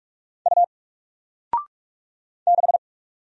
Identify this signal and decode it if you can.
Morse code, heard as a keyed tone, about 44 wpm, 700 Hz. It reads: U 6
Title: Single Letter-Number Alt 45wpm